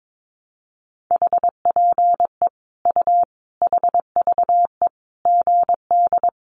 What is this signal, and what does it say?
Morse code, heard as a keyed tone, about 22 wpm, 700 Hz. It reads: HPE U HVE GD
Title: Common QSO Phrases 22wpm